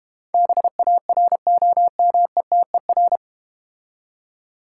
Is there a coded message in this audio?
Yes: Morse code from 0.3 to 3.2 s, heard as a keyed tone, about 32 wpm, 700 Hz. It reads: BAROMETER